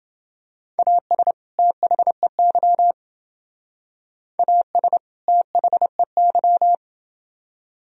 Morse code, heard as a keyed tone, about 28 wpm, 700 Hz. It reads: AS THEY AS THEY